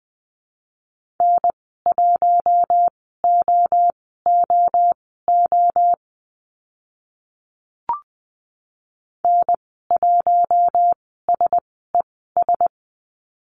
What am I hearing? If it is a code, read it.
Morse code, heard as a keyed tone, about 20 wpm, spaced as for 14 wpm, 700 Hz. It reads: N1OOO N1SES
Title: 042 Lesson 8g - Number 1 - Call Signs 20wpm